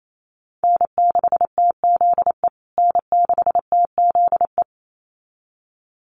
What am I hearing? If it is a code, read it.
Morse code, heard as a keyed tone, about 28 wpm, 700 Hz. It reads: N6TZE N6TZE
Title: USA Call Signs Repeated 2x 28wpm